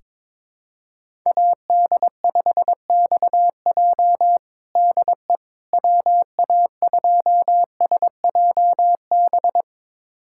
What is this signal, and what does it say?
Morse code, heard as a keyed tone, about 22 wpm, 700 Hz. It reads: AD5XJ DE WA2SJB
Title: USA Callsign DE Callsign 22wpm